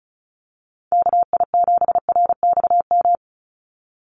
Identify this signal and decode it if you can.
Morse code, heard as a keyed tone, about 34 wpm, 700 Hz. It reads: KI7RXM
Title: USA Call Signs 35wpm